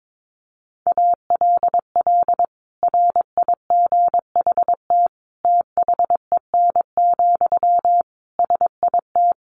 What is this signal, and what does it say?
Morse code, heard as a keyed tone, about 22 wpm, 700 Hz. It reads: ALL RIGHT THEN, SIT
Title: Sentences from Top 500 Words 22wpm